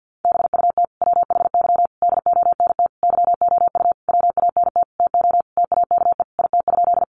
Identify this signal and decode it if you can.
Morse code, heard as a keyed tone, about 50 wpm, 700 Hz. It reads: BUT WHY DONT YOU WANT TO TAKE IT?
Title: Sentences from Top 300 Words 50wpm